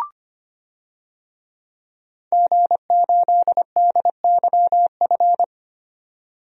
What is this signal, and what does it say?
Morse code, heard as a keyed tone, about 25 wpm, 700 Hz. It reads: G8DYF
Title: DXCC-100 Call Signs - 5 Characters - 25wpm